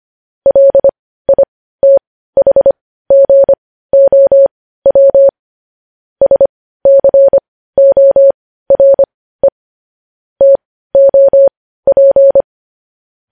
Morse code, heard as a keyed tone, about 25 wpm, spaced as for 15 wpm, 550 Hz. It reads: LITHGOW SCORE TOP
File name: podcast / News headlines-1x-25x15